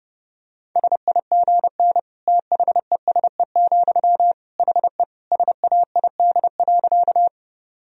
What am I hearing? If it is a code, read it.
Morse code, heard as a keyed tone, about 30 wpm, 700 Hz. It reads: SIGN THESE, HE SAID.
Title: Sentences from Top 1000 Words - Review Part 1 - 30wpm